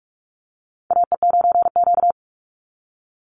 Morse code, heard as a keyed tone, about 44 wpm, 700 Hz. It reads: AE9Q
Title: NA SOTA Chaser Callsigns 45wpm